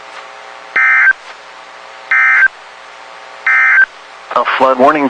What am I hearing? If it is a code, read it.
Emergency Alert System (SAME) tones, heard as harsh buzzing bursts, with speech. It is heard from 0.6 to 3.9 s.